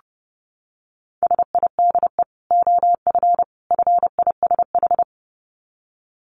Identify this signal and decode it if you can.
Morse code, heard as a keyed tone, about 30 wpm, 700 Hz. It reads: SIDE OF FISH
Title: Dit Challenge 30wpm